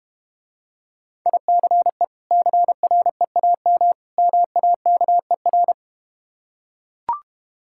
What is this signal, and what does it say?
Morse code, heard as a keyed tone, about 32 wpm, 700 Hz. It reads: ICE CREAM MAKER